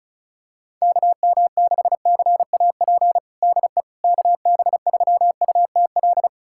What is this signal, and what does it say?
Morse code, heard as a keyed tone, about 36 wpm, 700 Hz. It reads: KM6CAP DE KB3UTL
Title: USA Callsign DE Callsign 35wpm